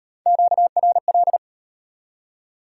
Morse code, heard as a keyed tone, about 34 wpm, 700 Hz. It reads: QRL